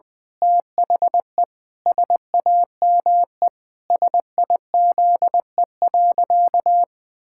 Morse code, heard as a keyed tone, about 19 wpm, 700 Hz. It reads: THE SAME SIZE.
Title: Pangram Challenge v2 - 20wpm